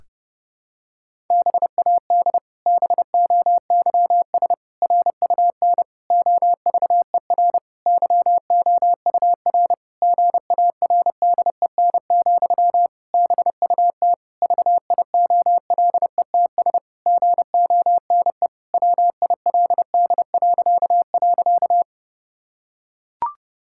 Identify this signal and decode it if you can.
Morse code, heard as a keyed tone, about 30 wpm, 700 Hz. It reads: BAD BOYS RUN OVER YOUR GARDEN, BUT VIOLETS GONE WILD..
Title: Pangram Challenge v2 - 30wpm